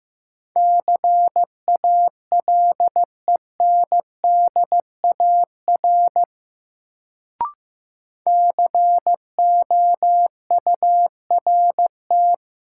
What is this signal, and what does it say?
Morse code, heard as a keyed tone, about 15 wpm, 700 Hz. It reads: CALENDAR COURT
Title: Sentences from Top 1600 Words - New Word Drill Alt - 15wpm